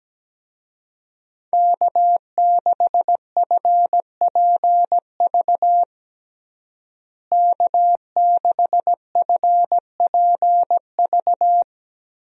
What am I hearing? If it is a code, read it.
Morse code, heard as a keyed tone, about 17 wpm, 700 Hz. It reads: K6FPV K6FPV
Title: USA Call Signs - 5 Characters - Repeated 3x - 17wpm